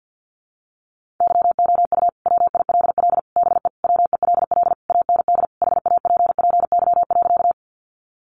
Morse code, heard as a keyed tone, about 52 wpm, 700 Hz. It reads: YOU WILL BE WELL AND HAPPY.
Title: Sentences from Top 1000 Words 50wpm